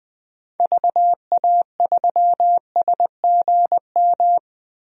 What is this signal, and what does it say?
Morse code, heard as a keyed tone, about 21 wpm, 700 Hz. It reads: VA3SGM
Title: DXCC-100 Call Signs - 6 Characters - Repeated 2x - 20wpm